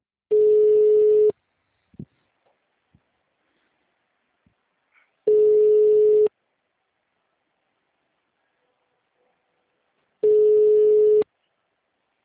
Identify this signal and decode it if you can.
Ringback tone (Europe, 425 Hz).